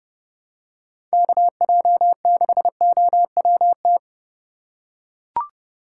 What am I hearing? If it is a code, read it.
Morse code, heard as a keyed tone, about 29 wpm, 700 Hz. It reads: KJ6OWT